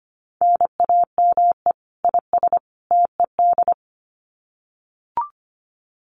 Morse code, heard as a keyed tone, about 25 wpm, 700 Hz. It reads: NAME IS TED
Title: Common QSO Phrases 25wpm